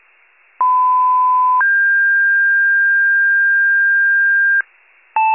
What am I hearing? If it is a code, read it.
Two-tone page, heard as a one-second tone, then a three-second tone, about 1010 then 1600 Hz. It is heard from 0.6 to 4.6 s.
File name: wcalert